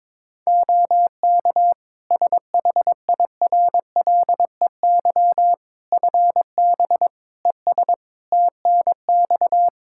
Morse code, heard as a keyed tone, about 22 wpm, 700 Hz. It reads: OK SHIRLEY FB ES TNX
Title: Practice QSO - Exchange 2 - Info - 22wpm